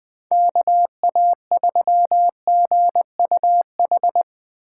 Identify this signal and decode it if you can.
Morse code, heard as a keyed tone, about 20 wpm, 700 Hz. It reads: KA3GUH
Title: Speed Racing - USA 2x3 Call Signs 17wpm